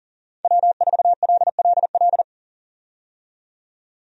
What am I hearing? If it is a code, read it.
Morse code, heard as a keyed tone, about 40 wpm, 700 Hz. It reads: W4LLL
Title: USA Call Signs - 5 Characters - 40wpm